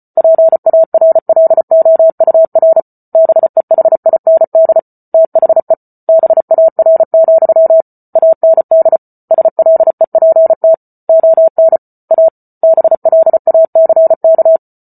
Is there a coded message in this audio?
Yes: Morse code from 0.2 to 14.6 s, heard as a keyed tone, about 34 wpm, 650 Hz. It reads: PARLOUR BEHIND THE BAR, AND SLEPT ON A BLACK